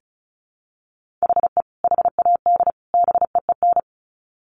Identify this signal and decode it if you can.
Morse code, heard as a keyed tone, about 34 wpm, 700 Hz. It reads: HE HAD BEEN
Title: Sets of 3 Words 35wpm